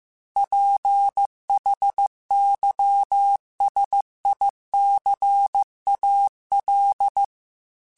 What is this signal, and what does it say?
Morse code, heard as a keyed tone, about 15 wpm, 800 Hz. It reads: PHYSICAL
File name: Technical Subjects / Technical Subjects-mp3s / Sw Pwr Sup @13